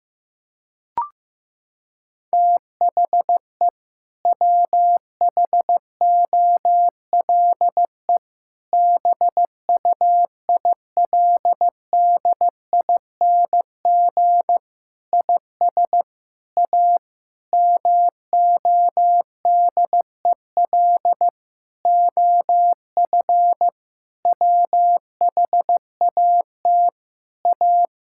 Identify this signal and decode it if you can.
Morse code, heard as a keyed tone, about 15 wpm, 700 Hz. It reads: THE WHOLE BUILDING IS A MODEL OF WHAT A
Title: Sentences from Top 1000 Words - Review Part 5 - 15wpm